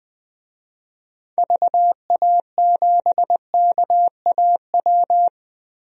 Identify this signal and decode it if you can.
Morse code, heard as a keyed tone, about 21 wpm, 700 Hz. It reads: VA7KAW
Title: DXCC-100 Call Signs - 6 Characters - Repeated 2x - 20wpm